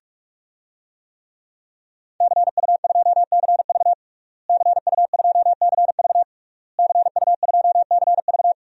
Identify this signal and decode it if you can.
Morse code, heard as a keyed tone, about 47 wpm, 700 Hz. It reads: CU2CV CU2CV CU2CV
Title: DXCC-100 Call Signs - 5 Characters - Repeated 3x - 45wpm